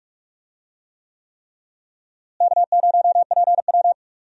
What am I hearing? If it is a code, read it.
Morse code, heard as a keyed tone, about 43 wpm, 700 Hz. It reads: K0PW
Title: USA Call Signs - 4 Characters - Repeated 3x - 45wpm